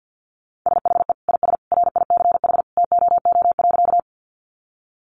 Morse code, heard as a keyed tone, about 48 wpm, 700 Hz. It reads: SHE IS RICH TOO.
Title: Sentences from Top 1100 Words 50wpm